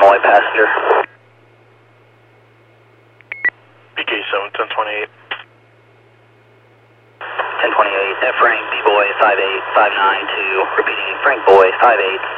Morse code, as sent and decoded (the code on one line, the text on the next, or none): none